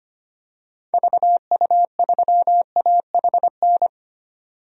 Morse code, heard as a keyed tone, about 25 wpm, 700 Hz. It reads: VU3AHN
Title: DXCC-100 Call Signs - 6 Characters - 25wpm